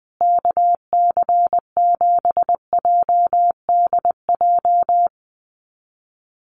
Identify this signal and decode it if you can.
Morse code, heard as a keyed tone, about 20 wpm, 700 Hz. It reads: KC7JDJ
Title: USA Call Signs Repeated 2x 20wpm